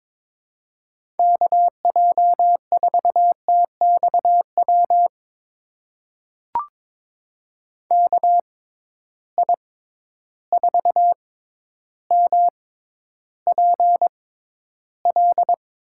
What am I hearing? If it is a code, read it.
Morse code, heard as a keyed tone, about 22 wpm, spaced as for 11 wpm, 700 Hz. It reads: KJ4TXW KI4MPL
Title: USA Call Signs - 6 Characters - 6x Character Spacing - 22wpm